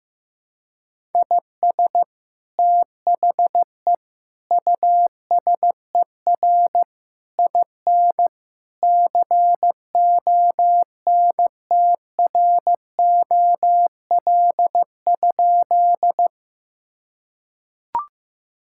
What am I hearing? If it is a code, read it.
Morse code, heard as a keyed tone, about 15 wpm, 700 Hz. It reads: IS THE USER IN CONTROL?
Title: Sentences from Top 1000 Words - Review Part 5 - 15wpm